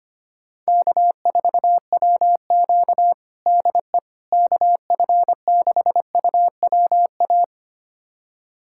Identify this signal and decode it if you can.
Morse code, heard as a keyed tone, about 25 wpm, 700 Hz. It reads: K4WQ DE KF6UWA
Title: USA Callsign DE Callsign 25wpm